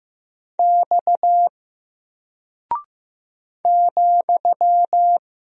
Morse code, heard as a keyed tone, about 15 wpm, 700 Hz. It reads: X ,
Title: Single Character (Letter:Number:Punctuation:Prosign) 15wpm